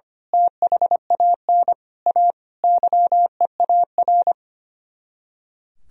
Morse code, heard as a keyed tone, about 25 wpm, 700 Hz. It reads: THAN A YEAR
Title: Speed-Racing - Sets of 5 Words 25wpm